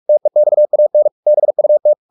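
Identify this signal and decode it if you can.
Morse code, heard as a keyed tone, about 44 wpm, 600 Hz. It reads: TEXAN BUT
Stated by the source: Morse practice at 45 wpm, the novel Never by Ken Follett in English, from CW med Gnister.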